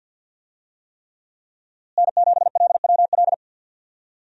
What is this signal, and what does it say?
Morse code, heard as a keyed tone, about 52 wpm, 700 Hz. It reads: N7LWL
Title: USA Call Signs - 5 Characters - Repeated 3x - 50wpm